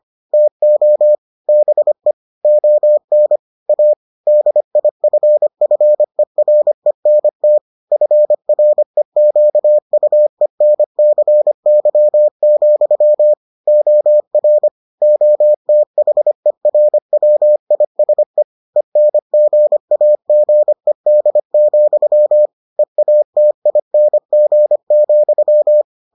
Morse code, heard as a keyed tone, about 25 wpm, 600 Hz. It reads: TO BE ON A DIFFERENT FREQUENCY, OR OTHERWISE ENGAGED, EATING,